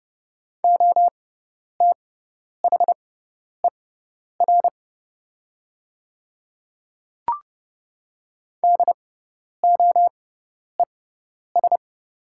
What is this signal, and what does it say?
Morse code, heard as a keyed tone, about 30 wpm, spaced as for 10 wpm, 700 Hz. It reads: OTHER DOES
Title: 6x Character Spacing - Top 500 Words - 30wpm